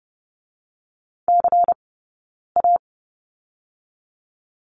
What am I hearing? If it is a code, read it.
Morse code, heard as a keyed tone, about 30 wpm, 700 Hz. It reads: C A
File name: 092 Lesson 16d - Letter c - Two Characters 30wpm